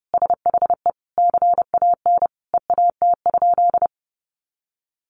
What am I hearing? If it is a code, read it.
Morse code, heard as a keyed tone, about 28 wpm, 700 Hz. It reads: SHE CAN EAT?
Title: Sentences from Top 500 Words 30wpm